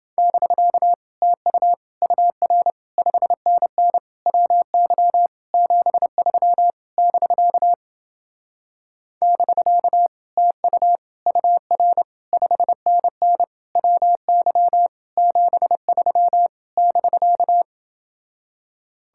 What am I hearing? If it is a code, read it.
Morse code, heard as a keyed tone, about 30 wpm, 700 Hz. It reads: <BK> TU UR 5NN WY 73 <BK> <BK> TU UR 5NN WY 73 <BK>